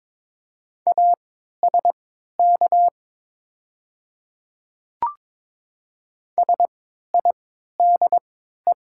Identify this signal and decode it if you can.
Morse code, heard as a keyed tone, about 22 wpm, spaced as for 12 wpm, 700 Hz. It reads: ASK SIDE
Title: 3x Character Spacing - Top 300 Words - 22wpm